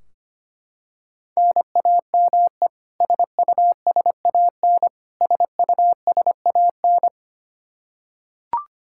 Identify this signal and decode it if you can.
Morse code, heard as a keyed tone, about 25 wpm, 700 Hz. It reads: NAME SUSAN SUSAN
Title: Practice QSO - Exchange 1 - Beginning - 25wpm